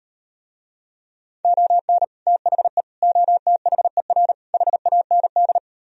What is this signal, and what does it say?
Morse code, heard as a keyed tone, about 39 wpm, 700 Hz. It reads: ON THE OTHER HAND